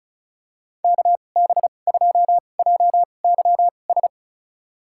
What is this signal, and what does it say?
Morse code, heard as a keyed tone, about 34 wpm, spaced as for 25 wpm, 690 Hz. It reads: KB2JYS
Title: USA Call Signs - 6 Characters - 2x Character Spacing - 35wpm